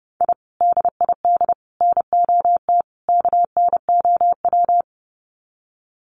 Morse code, heard as a keyed tone, about 30 wpm, 700 Hz. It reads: I DID NOT KNOW
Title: Sets of 4 Words 30wpm